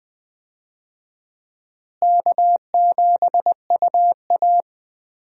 Morse code, heard as a keyed tone, about 20 wpm, 700 Hz. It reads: K7UA
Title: USA Call Signs - 4 Characters - Repeated 3x - 20wpm